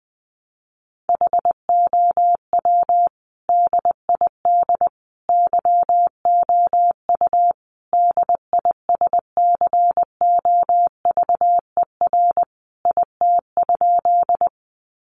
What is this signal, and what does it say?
Morse code, heard as a keyed tone, about 20 wpm, 700 Hz. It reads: HOW DID YOU DISCOVER IT?
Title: Sentences from Top 1100 Words 20wpm